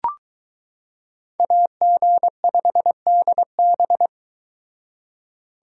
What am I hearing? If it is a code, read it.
Morse code, heard as a keyed tone, about 22 wpm, 700 Hz. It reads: AG5DB